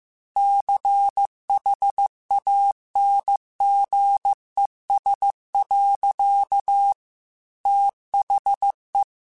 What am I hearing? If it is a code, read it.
Morse code, heard as a keyed tone, about 15 wpm, 800 Hz. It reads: CHANGES. THE
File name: Technical Subjects / Technical Subjects-mp3s / XL @13